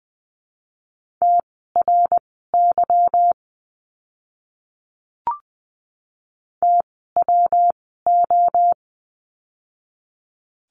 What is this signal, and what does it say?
Morse code, heard as a keyed tone, about 20 wpm, spaced as for 14 wpm, 700 Hz. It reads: TRY TWO
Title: 160 Lesson 26f - Letter g - Words 20wpm